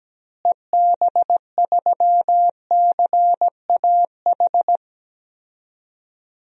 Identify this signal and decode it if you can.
Morse code, heard as a keyed tone, about 17 wpm, 700 Hz. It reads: EB3CAH
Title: DXCC-100 Call Signs - 6 Characters - Repeated 2x - 17wpm